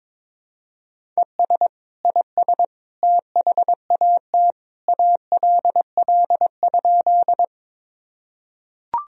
Morse code, heard as a keyed tone, about 22 wpm, 700 Hz. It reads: ES IS THAT ALL?
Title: Sentences from Top 100 Words and Abbreviations 22wpm